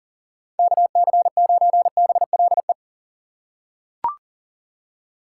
Morse code, heard as a keyed tone, about 40 wpm, 700 Hz. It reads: KC9BLE
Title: USA Call Signs - 6 Characters - 6x Character Spacing - 40wpm